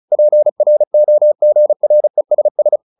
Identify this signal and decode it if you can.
Morse code, heard as a keyed tone, about 34 wpm, 600 Hz. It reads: PROGRESS
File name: the old man and the sea / the old man and the sea 1 35wpm